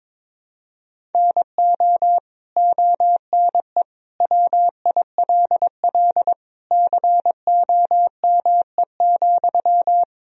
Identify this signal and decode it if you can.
Morse code, heard as a keyed tone, about 21 wpm, 700 Hz. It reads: NO ONE WILL COME,
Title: Sentences from Top 100 Words and Abbreviations 22wpm